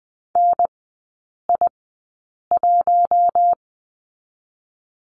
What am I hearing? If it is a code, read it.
Morse code, heard as a keyed tone, about 20 wpm, spaced as for 8 wpm, 700 Hz. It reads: NI1
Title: 041 Lesson 8e - Number 1 - Three Characters 20wpm